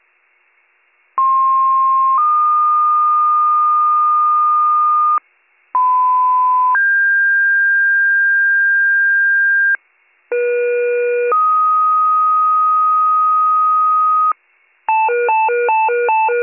Two-tone page, heard as a one-second tone, then a three-second tone, about 1060 then 1220 Hz, about 1010 then 1600 Hz, about 480 then 1250 Hz. It is heard from 1.2 to 14.3 s.